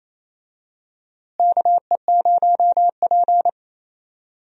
Morse code, heard as a keyed tone, about 28 wpm, 700 Hz. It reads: KE0P
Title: USA Call Signs - 4 Characters - Repeated 2x - 28wpm